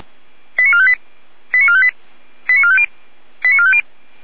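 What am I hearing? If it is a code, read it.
Five-tone selective call (ZVEI), heard as quick runs of five tones, tones of about 70 ms. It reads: 89458 89458 89450 89450
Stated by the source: ZVEI five-tone selective call received on an SDR.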